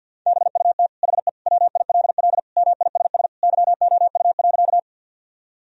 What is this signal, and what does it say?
Morse code, heard as a keyed tone, about 50 wpm, 700 Hz. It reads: BUT HE WILL MISS YOU.
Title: Sentences from Top 1000 Words - Review Part 3 - 50wpm